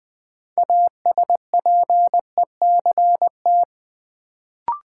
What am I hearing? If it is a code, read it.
Morse code, heard as a keyed tone, about 20 wpm, 700 Hz. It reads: ASPECT
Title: Sentences from Top 1300 Words - New Word Drill - 20wpm